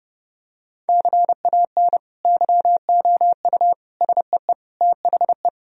Morse code, heard as a keyed tone, about 30 wpm, 700 Hz. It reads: CAN YOU SEE THE
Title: Sentences from Top 1000 Words - Review Part 1 - 30wpm